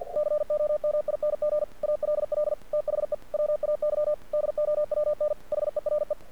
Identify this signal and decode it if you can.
Morse code, heard as a keyed tone, about 48 wpm, 600 Hz. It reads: COMING ALL THE WAY DOWN HERE
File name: archived / CRSnet